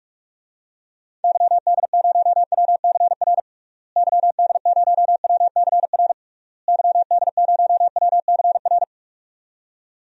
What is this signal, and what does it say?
Morse code, heard as a keyed tone, about 41 wpm, 700 Hz. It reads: YD0WCR YD0WCR YD0WCR
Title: DXCC-100 Call Signs - 6 Characters - Repeated 3x - 45wpm